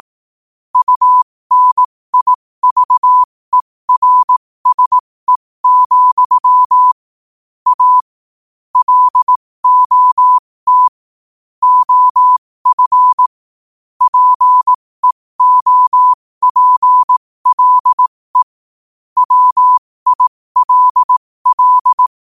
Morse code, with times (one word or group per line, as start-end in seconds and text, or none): 0.8-6.9 UNIVERSE,
7.7-8.0 A
8.8-10.9 LOT
11.6-13.3 OF
14.0-18.4 PEOPLE
19.2-22.1 WILL